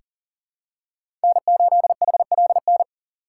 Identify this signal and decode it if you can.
Morse code, heard as a keyed tone, about 40 wpm, 700 Hz. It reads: N8HLN
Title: USA Call Signs - 5 Characters - 4x Character Spacing - 40wpm